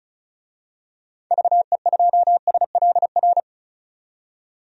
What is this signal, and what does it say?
Morse code, heard as a keyed tone, about 36 wpm, 700 Hz. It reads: VE2SLR
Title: DXCC-100 Call Signs - 6 Characters - 35wpm